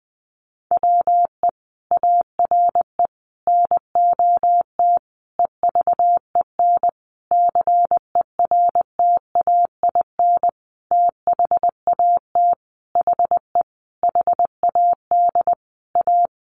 Morse code, heard as a keyed tone, about 20 wpm, 700 Hz. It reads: WE ARE NOT EVEN CERTAIN THAT HE HAD A
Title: Sentences from Top 1500 Words 20wpm